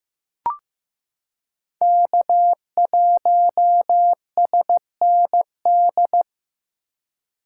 Morse code, heard as a keyed tone, about 15 wpm, 700 Hz. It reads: K1SND
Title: K1USN SST QSO Practice - 15wpm